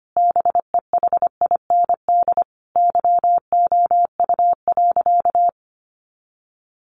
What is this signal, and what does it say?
Morse code, heard as a keyed tone, about 25 wpm, 700 Hz. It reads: BEHIND YOU.
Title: Sentences from Top 500 Words 25wpm